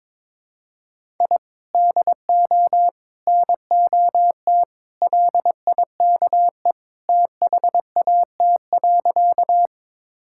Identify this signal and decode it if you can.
Morse code, heard as a keyed tone, about 22 wpm, 700 Hz. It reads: I DO NOT LIKE THAT.
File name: practice / 22wpm / Sentences with 3-Word Phrases - 22wpm